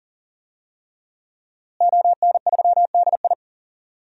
Morse code, heard as a keyed tone, about 40 wpm, 700 Hz. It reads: ON3DI
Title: DXCC-100 Call Signs - 5 Characters - Repeated 3x - 40wpm